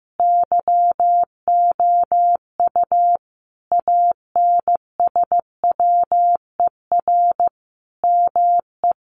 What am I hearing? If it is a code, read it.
Morse code, heard as a keyed tone, about 15 wpm, 700 Hz. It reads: YOU ANSWER ME
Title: Sentences from Top 1500 Words 15wpm